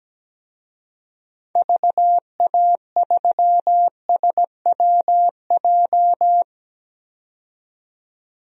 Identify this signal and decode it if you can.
Morse code, heard as a keyed tone, about 17 wpm, 700 Hz. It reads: VA3SWJ
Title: DXCC-100 Call Signs - 6 Characters - 17wpm